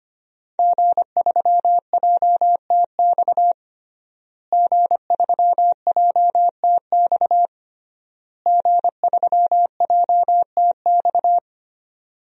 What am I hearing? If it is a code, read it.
Morse code, heard as a keyed tone, about 25 wpm, 700 Hz. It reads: G3JTX G3JTX G3JTX
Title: DXCC-100 Call Signs - 5 Characters - Repeated 3x - 25wpm